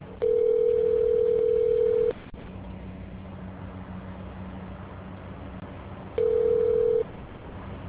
Ringback tone (North America), 440 + 480 Hz.